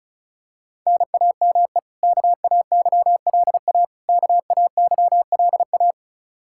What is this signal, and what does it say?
Morse code, heard as a keyed tone, about 34 wpm, 700 Hz. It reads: NAME KAYLA KAYLA
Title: Practice QSO - Exchange 1 - Beginning - 35wpm